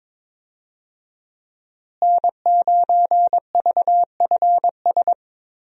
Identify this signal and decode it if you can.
Morse code, heard as a keyed tone, about 21 wpm, 700 Hz. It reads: N9VFS
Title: USA Call Signs - 5 Characters - Repeated 2x - 22wpm